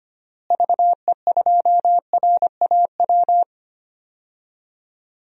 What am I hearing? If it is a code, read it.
Morse code, heard as a keyed tone, about 25 wpm, 700 Hz. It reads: VE2RAW
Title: DXCC-100 Call Signs - 6 Characters - Repeated 3x - 25wpm